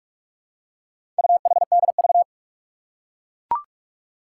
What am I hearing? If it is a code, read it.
Morse code, heard as a keyed tone, about 44 wpm, 700 Hz. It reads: UHDV
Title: Four Challenging Letter-Number Combinations 45wpm